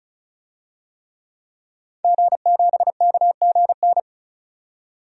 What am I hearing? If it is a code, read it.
Morse code, heard as a keyed tone, about 36 wpm, 700 Hz. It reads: G7KGN
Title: DXCC-100 Call Signs - 5 Characters - 35wpm